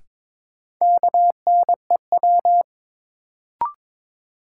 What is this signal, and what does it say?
Morse code, heard as a keyed tone, about 22 wpm, 700 Hz. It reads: KNEW
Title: Sentences from Top 400 Words - New Word Drill - 22wpm